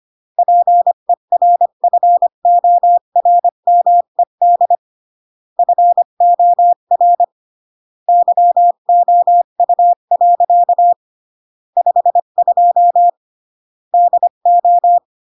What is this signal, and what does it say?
Morse code, heard as a keyed tone, about 25 wpm, 700 Hz. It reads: PERFORMED FOR YOU. 52 DO